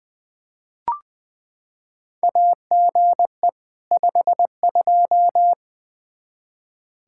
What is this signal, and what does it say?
Morse code, heard as a keyed tone, about 20 wpm, 700 Hz. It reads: AGE 52
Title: Practice QSO - Exchange 3 - More Info - 20wpm